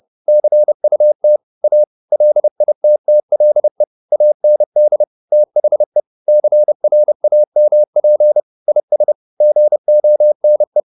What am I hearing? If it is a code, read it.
Morse code, heard as a keyed tone, about 30 wpm, 600 Hz. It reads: CUT A LITTLE AND THE CRAMP IS GONE